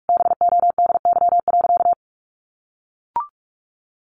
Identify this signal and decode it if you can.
Morse code, heard as a keyed tone, about 44 wpm, 700 Hz. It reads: BODY.
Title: A Sound Of Thunder 45wpm